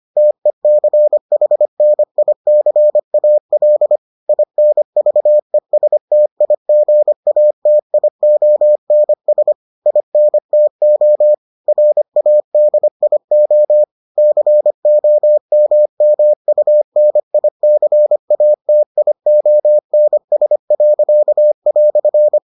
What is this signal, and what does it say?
Morse code, heard as a keyed tone, about 25 wpm, 600 Hz. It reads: TECHNICAL INVESTIGATIONS INTO RADIO COMMUNICATIONS."